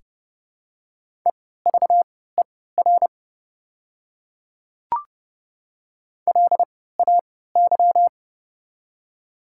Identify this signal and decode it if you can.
Morse code, heard as a keyed tone, about 29 wpm, spaced as for 17 wpm, 700 Hz. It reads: EVER LAY
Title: 3x Character Spacing - Top 500 Words - 30wpm